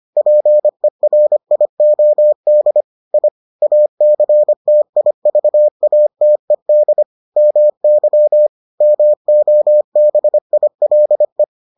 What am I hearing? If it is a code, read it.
Morse code, heard as a keyed tone, about 25 wpm, 600 Hz. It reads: PERIOD I ACTIVATED MY MOBILE